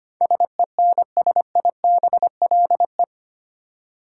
Morse code, heard as a keyed tone, about 25 wpm, 700 Hz. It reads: SENSIBLE
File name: Eight Letter Words - 25wpm